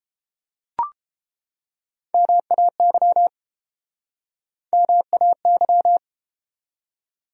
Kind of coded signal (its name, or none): Morse code